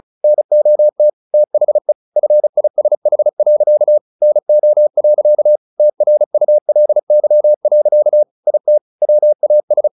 Morse code, heard as a keyed tone, about 34 wpm, 600 Hz. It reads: NOT THE FISH. NO. TRULY. IT WAS
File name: the old man and the sea 2 35wpm